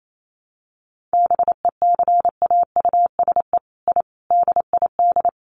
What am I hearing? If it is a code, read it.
Morse code, heard as a keyed tone, about 28 wpm, 700 Hz. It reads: BECAUSE I DID
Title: Sets of 5 Words Encore2 28wpm